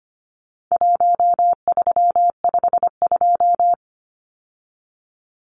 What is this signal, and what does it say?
Morse code, heard as a keyed tone, about 25 wpm, 700 Hz. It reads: 1352
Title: Numbers - Contest Style - 4-Digit 25wpm